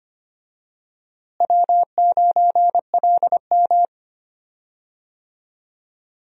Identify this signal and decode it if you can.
Morse code, heard as a keyed tone, about 25 wpm, 700 Hz. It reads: W9LM
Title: USA Call Signs - 4 Characters - Repeated 2x - 25wpm